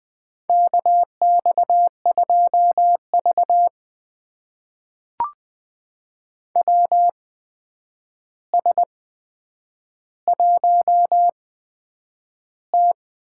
Morse code, heard as a keyed tone, about 20 wpm, spaced as for 9 wpm, 700 Hz. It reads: KX2V WS1T
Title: USA Call Signs - 4 Characters - 8x Character Spacing - 20wpm